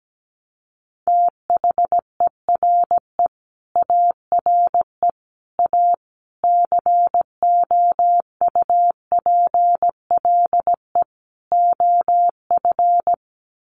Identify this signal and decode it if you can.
Morse code, heard as a keyed tone, about 17 wpm, 690 Hz. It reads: THERE ARE A COUPLE OF
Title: Sets of 5 Words Encore 17wpm